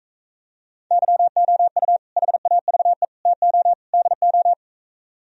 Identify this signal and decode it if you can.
Morse code, heard as a keyed tone, about 44 wpm, 700 Hz. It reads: YOU HAVE TO DO